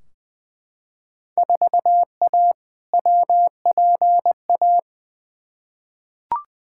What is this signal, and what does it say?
Morse code, heard as a keyed tone, about 20 wpm, 700 Hz. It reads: 4A WPA
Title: Field Day Exchanges Encore - 20wpm